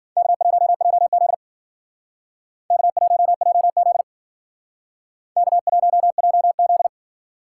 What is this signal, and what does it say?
Morse code, heard as a keyed tone, about 52 wpm, 700 Hz. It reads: K1JZ K1JZ K1JZ